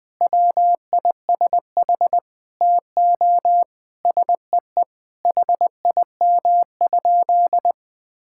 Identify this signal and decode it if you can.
Morse code, heard as a keyed tone, about 20 wpm, 700 Hz. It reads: WISH TO SEE HIM?
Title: Sentences from Top 1600 Words 20wpm